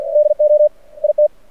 Morse code, heard as a keyed tone, about 50 wpm, 600 Hz. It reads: DO IT
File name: archived / CRSnet